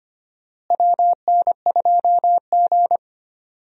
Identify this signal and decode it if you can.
Morse code, heard as a keyed tone, about 25 wpm, 700 Hz. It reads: WN2G
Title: USA Call Signs - 4 Characters - Repeated 2x - 25wpm